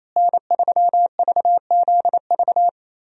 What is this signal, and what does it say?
Morse code, heard as a keyed tone, about 27 wpm, 700 Hz. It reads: N3VZV
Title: USA Call Signs - 5 Characters - Repeated 2x - 28wpm